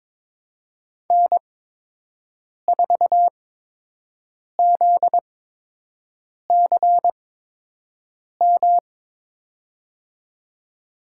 Morse code, heard as a keyed tone, about 22 wpm, spaced as for 6 wpm, 700 Hz. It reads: N4ZCM
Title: USA Call Signs - 5 Characters - 8x Character Spacing - 22wpm